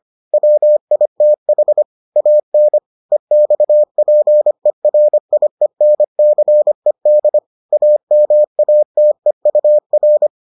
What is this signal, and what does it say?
Morse code, heard as a keyed tone, about 25 wpm, 600 Hz. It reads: WITH AN EXPERIENCED AMATEUR